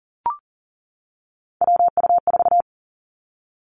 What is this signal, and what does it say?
Morse code, heard as a keyed tone, about 39 wpm, 700 Hz. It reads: WU4
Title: Three Challenging Letter-Number Combinations 40wpm